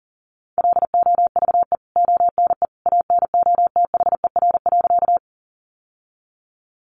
Morse code, heard as a keyed tone, about 40 wpm, 700 Hz. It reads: LOVE ONE ANOTHER.
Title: Sentences from Top 600 Words 40wpm